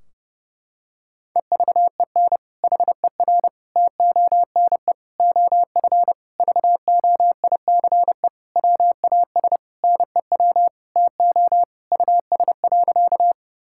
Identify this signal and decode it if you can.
Morse code, heard as a keyed tone, about 32 wpm, 700 Hz. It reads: EVEN HER TONE OF VOICE WAS NEW TO US.
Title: Sentences from Top 1600 Words 30wpm